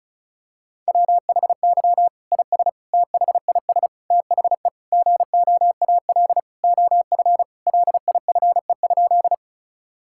Morse code, heard as a keyed tone, about 34 wpm, 700 Hz. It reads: WHY IS THIS THE GOAL OF LIFE?
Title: Sentences from Top 1000 Words - Review Part 5 - 35wpm